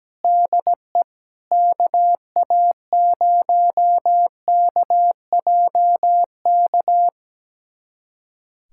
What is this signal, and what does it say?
Morse code, heard as a keyed tone, about 17 wpm, 700 Hz. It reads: DE KA0KJK
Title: USA Callsign DE Callsign - 6 Characters - 17wpm